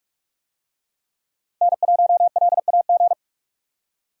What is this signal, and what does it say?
Morse code, heard as a keyed tone, about 45 wpm, 700 Hz. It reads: N1LAG